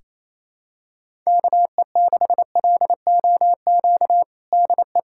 Morse code, heard as a keyed tone, about 28 wpm, 700 Hz. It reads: KE6LOQ DE
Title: USA Callsign DE Callsign 28wpm